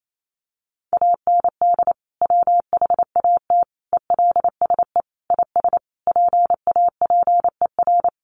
Morse code, heard as a keyed tone, about 28 wpm, 700 Hz. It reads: AND WHAT ELSE IS PAPER
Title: Sentences from Top 1500 Words 28wpm